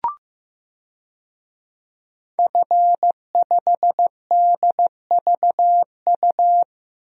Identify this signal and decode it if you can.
Morse code, heard as a keyed tone, about 15 wpm, 700 Hz. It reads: F5DVU
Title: DXCC-100 Call Signs - 5 Characters - 15wpm